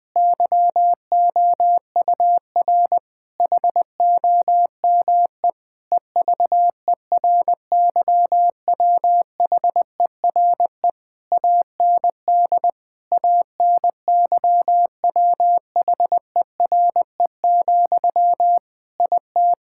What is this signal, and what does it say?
Morse code, heard as a keyed tone, about 19 wpm, 700 Hz. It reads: YOUR HOME EVERYWHERE AND ANYWHERE, IT
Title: Sentences from Top 1600 Words 20wpm